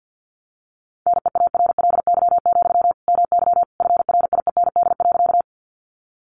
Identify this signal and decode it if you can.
Morse code, heard as a keyed tone, about 50 wpm, 700 Hz. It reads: NEARLY, MY FRIEND.
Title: Sentences from Top 1300 Words 50wpm